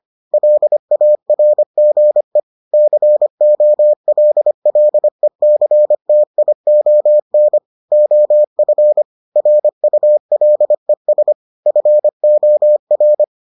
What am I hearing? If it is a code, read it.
Morse code, heard as a keyed tone, about 25 wpm, 600 Hz. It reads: LARGE COLLECTION OF RULES FOR